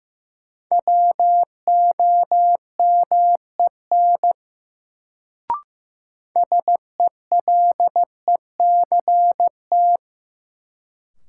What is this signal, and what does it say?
Morse code, heard as a keyed tone, about 16 wpm, 700 Hz. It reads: WOMEN SELECT
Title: Sentences from Top 1300 Words - New Word Drill - 15wpm